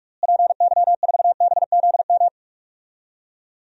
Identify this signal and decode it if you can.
Morse code, heard as a keyed tone, about 46 wpm, 700 Hz. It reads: PY4BZM
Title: DXCC-100 Call Signs - 6 Characters - Repeated 3x - 45wpm